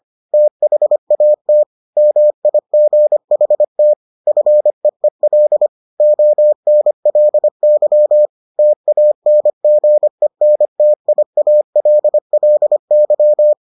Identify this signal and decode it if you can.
Morse code, heard as a keyed tone, about 25 wpm, 600 Hz. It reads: THAT MIGHT FEEL ONLY TANGENTIALLY